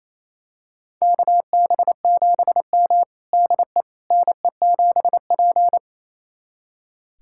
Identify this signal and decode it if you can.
Morse code, heard as a keyed tone, about 28 wpm, 700 Hz. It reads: KB7M DE NE7P
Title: USA Callsign DE Callsign - 4 Characters - 28wpm